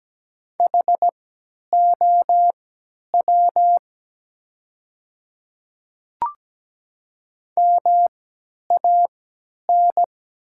Morse code, heard as a keyed tone, about 17 wpm, spaced as for 10 wpm, 700 Hz. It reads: HOW MAN
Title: 3x Character Spacing - Top 300 Words - 17wpm